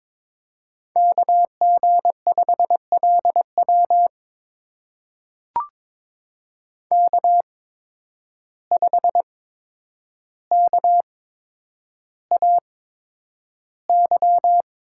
Morse code, heard as a keyed tone, about 22 wpm, spaced as for 9 wpm, 700 Hz. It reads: KG5LW K5KAY
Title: USA Call Signs - 5 Characters - 8x Character Spacing - 22wpm